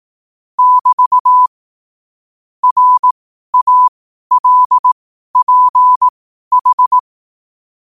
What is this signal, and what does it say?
Morse code, heard as a keyed tone, about 18 wpm, spaced as for 12 wpm, 1000 Hz. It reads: <BT> RALPH
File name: qotdfiles / 12wpm